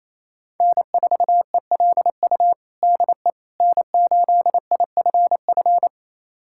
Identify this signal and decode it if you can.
Morse code, heard as a keyed tone, about 27 wpm, 700 Hz. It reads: N4ELU DE N8IFF
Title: USA Callsign DE Callsign - 5 Characters - 28wpm